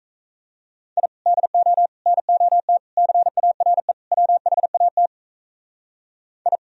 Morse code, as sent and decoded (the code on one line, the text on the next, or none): .. / -.. --- / -. --- - / -.-. .- .-. / .-- .... .- - / ..
I DO NOT CARE WHAT I